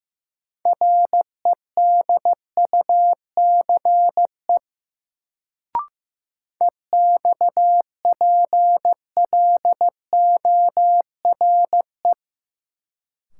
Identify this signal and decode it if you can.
Morse code, heard as a keyed tone, about 15 wpm, 700 Hz. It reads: REDUCE EXPLORE
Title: Sentences from Top 1100 Words - New Word Drill - 15wpm